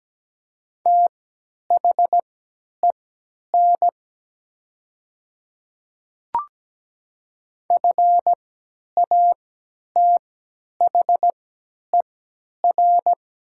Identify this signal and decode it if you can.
Morse code, heard as a keyed tone, about 17 wpm, spaced as for 10 wpm, 700 Hz. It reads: THEN FATHER
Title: 3x Character Spacing - Top 500 Words - 17wpm